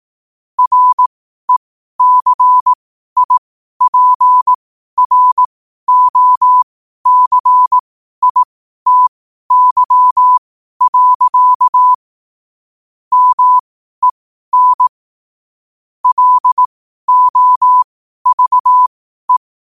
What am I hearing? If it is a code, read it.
Morse code, heard as a keyed tone, about 18 wpm, spaced as for 12 wpm, 1000 Hz. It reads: RECIPROCITY. MEN LOVE
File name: qotdfiles / 12wpm